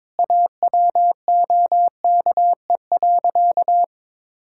Morse code, heard as a keyed tone, about 23 wpm, 700 Hz. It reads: AWOKE.
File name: practice / 22wpm / Sentences with 4-Word Phrases - 22wpm